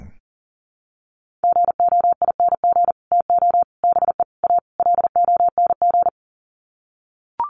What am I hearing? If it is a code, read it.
Morse code, heard as a keyed tone, about 40 wpm, 700 Hz. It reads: GOING TO BE A LONG
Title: Sets of 5 Words Encore2 40wpm